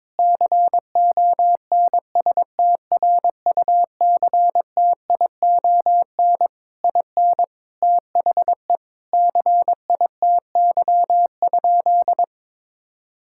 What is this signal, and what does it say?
Morse code, heard as a keyed tone, about 22 wpm, 700 Hz. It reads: CONSTRUCTION IN THE CITY?
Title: Sentences from Top 1600 Words 22wpm